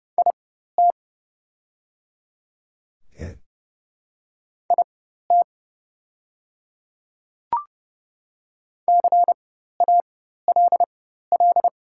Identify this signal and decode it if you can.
Morse code, heard as a keyed tone, about 30 wpm, spaced as for 14 wpm, 700 Hz. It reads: IT IT CALL
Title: 4x Character Spacing - Top 100 Words - 30wpm